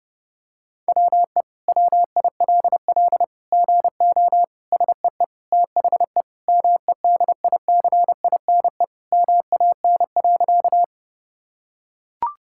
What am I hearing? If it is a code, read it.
Morse code, heard as a keyed tone, about 32 wpm, 700 Hz. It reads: WE WILL GO SEE THE MEDICINE MAN.
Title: Sentences from Top 1000 Words - Review Part 6 - 30wpm